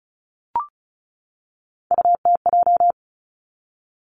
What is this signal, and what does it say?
Morse code, heard as a keyed tone, about 36 wpm, 700 Hz. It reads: UTJ
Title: Three Challenging Letter-Number Combinations 35wpm